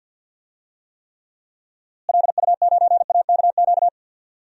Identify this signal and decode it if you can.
Morse code, heard as a keyed tone, about 51 wpm, 700 Hz. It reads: LU9AKQ
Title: DXCC-100 Call Signs - 6 Characters - Repeated 3x - 50wpm